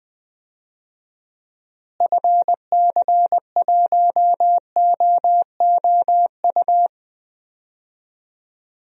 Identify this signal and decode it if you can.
Morse code, heard as a keyed tone, about 20 wpm, 700 Hz. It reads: FC1OOU
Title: DXCC-100 Call Signs - 6 Characters - Repeated 3x - 20wpm